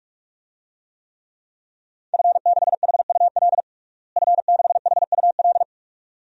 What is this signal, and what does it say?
Morse code, heard as a keyed tone, about 46 wpm, 700 Hz. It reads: F6HUL F6HUL
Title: DXCC-100 Call Signs - 5 Characters - Repeated 2x - 45wpm